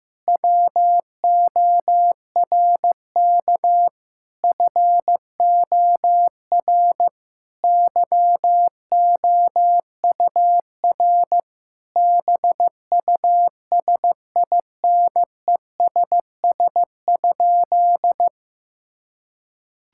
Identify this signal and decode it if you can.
Morse code, heard as a keyed tone, about 15 wpm, 700 Hz. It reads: WORK FOR YOUR BUSINESS?
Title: Sentences from Top 1000 Words - Review Part 6 - 15wpm